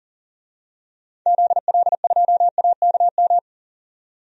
Morse code, heard as a keyed tone, about 40 wpm, 700 Hz. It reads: ZL2AKM
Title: DXCC-100 Call Signs - 6 Characters - 40wpm